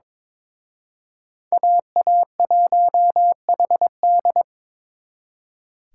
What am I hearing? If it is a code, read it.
Morse code, heard as a keyed tone, about 26 wpm, 700 Hz. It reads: AA1HD
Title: Speed Racing - USA 2x2 Call Signs 22wpm